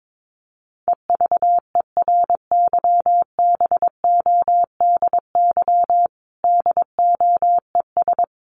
Morse code, heard as a keyed tone, about 22 wpm, 700 Hz. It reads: EVERYBODY DOES
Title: Sentences from Top 1400 Words 22wpm